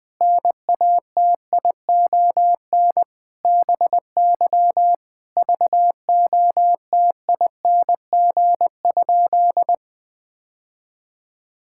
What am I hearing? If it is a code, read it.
Morse code, heard as a keyed tone, about 20 wpm, 700 Hz. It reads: NATION BY VOTING?
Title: Sentences with 9-Word Phrases - 20wpm